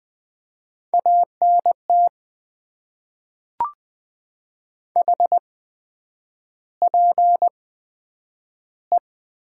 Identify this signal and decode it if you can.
Morse code, heard as a keyed tone, about 20 wpm, spaced as for 9 wpm, 700 Hz. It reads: ANT HPE
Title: Top 90 QSO Elements - 8x Character Spacing - 20wpm